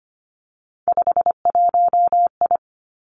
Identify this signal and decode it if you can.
Morse code, heard as a keyed tone, about 25 wpm, 700 Hz. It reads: 51I
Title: Three Challenging Letter-Number Combinations 25wpm